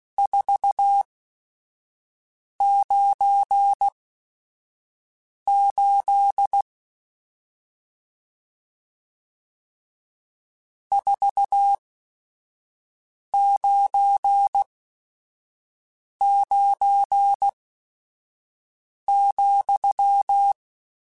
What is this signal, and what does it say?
Morse code, heard as a keyed tone, about 16 wpm, spaced as for 5 wpm, 800 Hz. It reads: 498 499,
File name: RSTs / RST-mp3s / Various RST @5